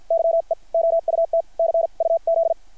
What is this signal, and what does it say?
Morse code, heard as a keyed tone, about 50 wpm, spaced as for 43 wpm, 660 Hz. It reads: YE OUT CUZ
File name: archived / CRSnet